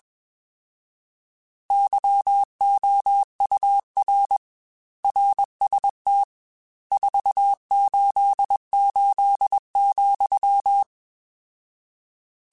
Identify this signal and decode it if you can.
Morse code, heard as a keyed tone, about 21 wpm, 800 Hz. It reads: YOUR RST 488,